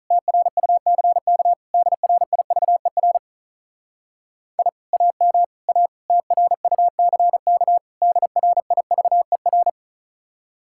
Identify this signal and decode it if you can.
Morse code, heard as a keyed tone, about 39 wpm, 700 Hz. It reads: TRUCK DRIVER I AM A TRUCK DRIVER
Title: Speed-Racing Common QSO Phrases Encore - 35wpm